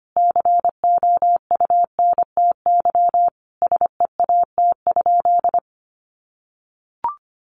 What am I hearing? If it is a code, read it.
Morse code, heard as a keyed tone, about 25 wpm, 700 Hz. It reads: COUNTY SEAT?
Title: Sentences from Top 1500 Words 25wpm